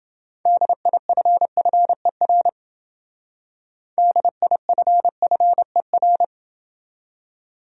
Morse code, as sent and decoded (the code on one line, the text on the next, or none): -.. .. ..-. ..-. .-. / -.. .. ..-. ..-. .-.
DIFFER DIFFER